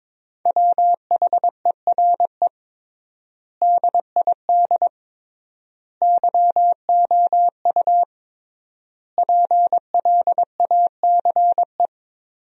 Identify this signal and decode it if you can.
Morse code, heard as a keyed tone, about 22 wpm, 700 Hz. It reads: WHERE DID YOU PLACE
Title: 3x Word-Spacing - Sentences from Top 200 Words - 22wpm